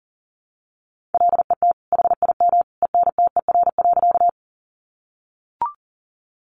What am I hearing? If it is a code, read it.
Morse code, heard as a keyed tone, about 40 wpm, 700 Hz. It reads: LET HIM ENTER.
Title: Sentences from Top 1300 Words 40wpm